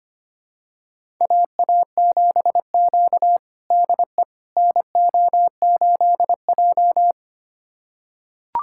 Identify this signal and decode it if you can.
Morse code, heard as a keyed tone, about 25 wpm, 700 Hz. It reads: AA7Q DE NO8J
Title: USA Callsign DE Callsign - 4 Characters - 25wpm